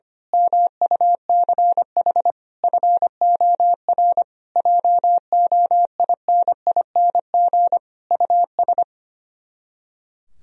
Morse code, heard as a keyed tone, about 24 wpm, 700 Hz. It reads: MUCH FOR JOINING US
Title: Speed-Racing - Sets of 5 Words 25wpm